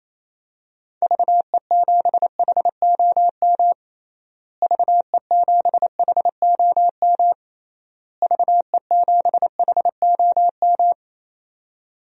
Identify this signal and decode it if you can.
Morse code, heard as a keyed tone, about 28 wpm, 700 Hz. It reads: VE7HOM VE7HOM VE7HOM
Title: DXCC-100 Call Signs - 6 Characters - Repeated 3x - 28wpm